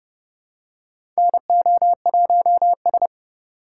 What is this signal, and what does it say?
Morse code, heard as a keyed tone, about 30 wpm, 700 Hz. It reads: NO1S
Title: USA Call Signs - 4 Characters - 3x Character Spacing - 30wpm